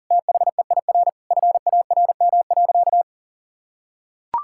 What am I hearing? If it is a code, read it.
Morse code, heard as a keyed tone, about 40 wpm, 700 Hz. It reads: THEIR FARM.
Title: Sentences from Top 1000 Words - Review Part 6 - 40wpm